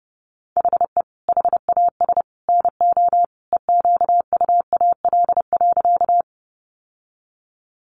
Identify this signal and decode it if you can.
Morse code, heard as a keyed tone, about 30 wpm, 700 Hz. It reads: HE HAS NO EQUAL.
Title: Sentences from Top 1400 Words 30wpm